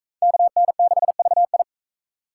Morse code, heard as a keyed tone, about 40 wpm, 700 Hz. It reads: KN6VI